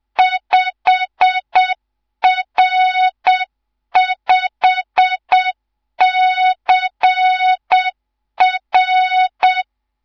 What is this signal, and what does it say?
Morse code, heard as a keyed tone, about 7 wpm, 3700 Hz. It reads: 5R5CR
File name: Morze-HA2MN / hang / 11 GYAKORLAT